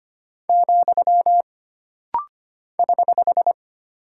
Morse code, heard as a keyed tone, about 25 wpm, 700 Hz. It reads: , <HH>
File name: Single Punctuation-Prosign - Rapid-Fire - 25wpm